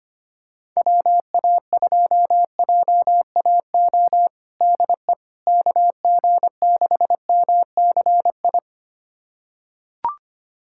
Morse code, heard as a keyed tone, about 25 wpm, 700 Hz. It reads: WA2JAO DE KG6MCI
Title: USA Callsign DE Callsign 25wpm